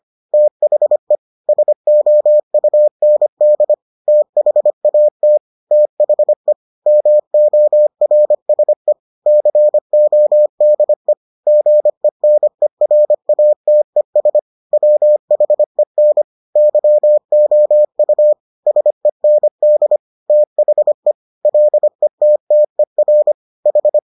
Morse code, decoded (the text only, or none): THE SOUND THAT THE MORSE CODE GENERATES WHEN YOU SEND THE LETTER H